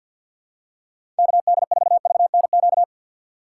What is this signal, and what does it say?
Morse code, heard as a keyed tone, about 52 wpm, 700 Hz. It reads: KD4VNQ
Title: USA Call Signs - 6 Characters - 6x Character Spacing - 50wpm